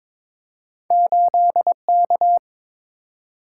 Morse code, heard as a keyed tone, about 22 wpm, 700 Hz. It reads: 8K
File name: practice / 22wpm / Two Letter-Number Combinations - 22wpm